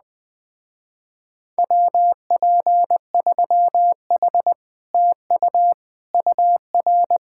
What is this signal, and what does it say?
Morse code, heard as a keyed tone, about 20 wpm, 700 Hz. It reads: WP3H TU UR
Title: Speed-Racing Common QSO Phrases Encore - 17wpm